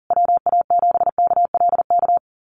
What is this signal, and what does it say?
Morse code, heard as a keyed tone, about 40 wpm, 700 Hz. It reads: WA7KLK
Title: USA Call Signs Repeated 3x 40wpm